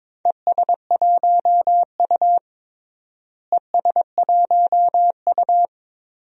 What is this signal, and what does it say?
Morse code, heard as a keyed tone, about 22 wpm, 700 Hz. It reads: ES1U ES1U
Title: DXCC-100 Call Signs - 4 Characters - Repeated 3x - 22wpm